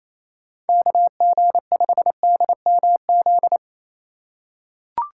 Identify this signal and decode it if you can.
Morse code, heard as a keyed tone, about 28 wpm, 700 Hz. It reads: KG5DMZ
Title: USA Call Signs - 6 Characters - 3x Character Spacing - 28wpm